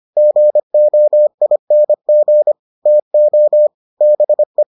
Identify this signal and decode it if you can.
Morse code, heard as a keyed tone, about 25 wpm, 600 Hz. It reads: GOING TO BE